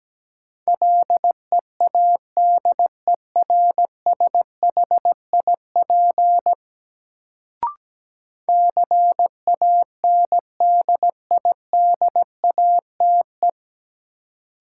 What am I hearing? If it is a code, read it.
Morse code, heard as a keyed tone, about 17 wpm, 700 Hz. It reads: LEADERSHIP CANDIDATE
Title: Sentences from Top 1600 Words - New Word Drill - 17wpm